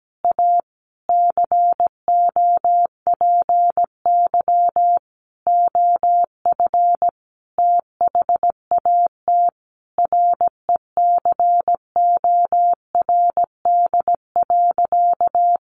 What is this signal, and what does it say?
Morse code, heard as a keyed tone, about 17 wpm, 700 Hz. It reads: A COPY OF THAT RECORD.
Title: Sentences from Top 1400 Words 17wpm